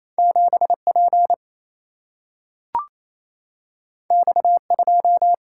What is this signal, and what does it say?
Morse code, heard as a keyed tone, about 28 wpm, 700 Hz. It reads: 7 P X 2
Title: Single-Letter Single-Number 28wpm